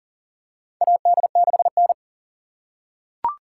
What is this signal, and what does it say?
Morse code, heard as a keyed tone, about 40 wpm, 700 Hz. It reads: AD6N
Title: USA Call Signs - 4 Characters - 6x Character Spacing - 40wpm